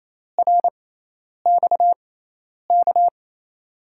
Morse code, heard as a keyed tone, about 28 wpm, spaced as for 10 wpm, 700 Hz. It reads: RXK
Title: USA Call Signs - 6x Character Spacing - 28wpm